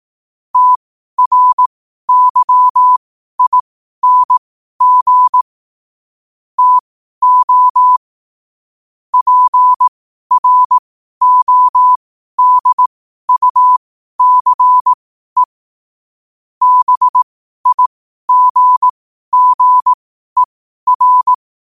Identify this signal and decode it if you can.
Morse code, heard as a keyed tone, about 18 wpm, spaced as for 12 wpm, 1000 Hz. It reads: TRYING TO PRODUCE BIGGER
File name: qotdfiles / 12wpm